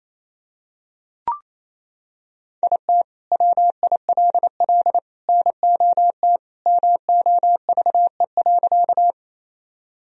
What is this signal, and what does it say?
Morse code, heard as a keyed tone, about 28 wpm, 700 Hz. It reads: IT WILL NOT MOVE.
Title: Sentences from Top 1000 Words - Review Part 1 - 28wpm